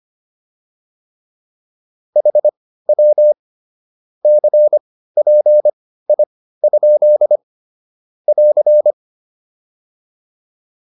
Morse code, heard as a keyed tone, about 25 wpm, spaced as for 15 wpm, 600 Hz. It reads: HW CPI? <AR>
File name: QSO208 15